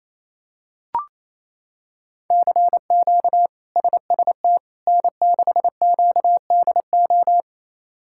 Morse code, heard as a keyed tone, about 28 wpm, 700 Hz. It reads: CQ SST N6QDO
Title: K1USN SST QSO Practice - 28wpm